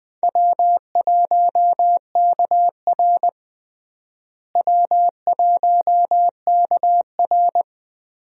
Morse code, heard as a keyed tone, about 20 wpm, 700 Hz. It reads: W1KR W1KR
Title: USA Call Signs - 4 Characters - Repeated 3x - 20wpm